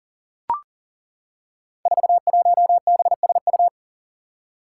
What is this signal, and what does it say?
Morse code, heard as a keyed tone, about 40 wpm, 700 Hz. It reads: 41BSU
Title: Five Challenging Letter-Number Combinations 40wpm